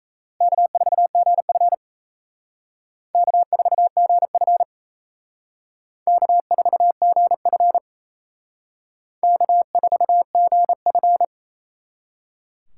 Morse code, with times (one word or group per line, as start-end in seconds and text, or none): 0.4-1.7 K4GF
3.1-4.6 K4GF
6.1-7.8 K4GF
9.2-11.2 K4GF